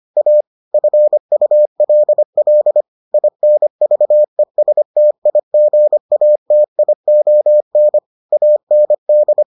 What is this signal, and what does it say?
Morse code, heard as a keyed tone, about 25 wpm, 600 Hz. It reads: A FULL INVESTIGATION AND